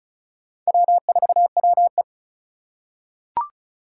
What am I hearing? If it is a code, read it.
Morse code, heard as a keyed tone, about 36 wpm, 700 Hz. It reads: W4WE
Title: Four Challenging Letter-Number Combinations 35wpm